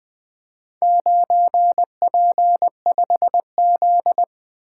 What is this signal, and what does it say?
Morse code, heard as a keyed tone, about 20 wpm, 700 Hz. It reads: 9P5Z
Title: Four Letter-Number Combinations - 20wpm